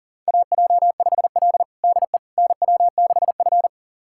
Morse code, heard as a keyed tone, about 40 wpm, 700 Hz. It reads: AJ5L DE NW6F
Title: USA Callsign DE Callsign - 4 Characters - 40wpm